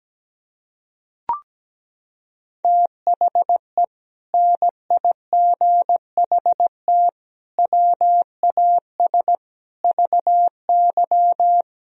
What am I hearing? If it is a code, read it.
Morse code, heard as a keyed tone, about 17 wpm, 700 Hz. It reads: THE NIGHT WAS VY
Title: Sentences from Top 500 Words and Abbreviations 17wpm